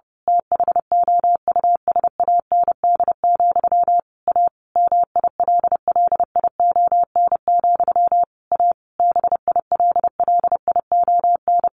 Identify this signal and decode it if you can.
Morse code, heard as a keyed tone, about 28 wpm, 700 Hz. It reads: THOUSAND, A MILLION, A BILLION
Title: A Sound Of Thunder 30wpm